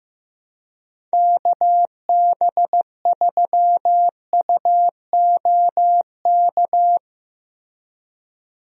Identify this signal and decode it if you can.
Morse code, heard as a keyed tone, about 16 wpm, 700 Hz. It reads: KB3UOK
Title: USA Call Signs - 6 Characters - Repeated 3x - 15wpm